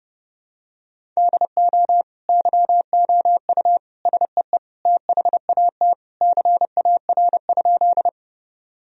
Morse code, heard as a keyed tone, about 30 wpm, 700 Hz. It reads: DO YOU SEE THAT CAR?
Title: Sentences from Top 1000 Words - Review Part 1 - 30wpm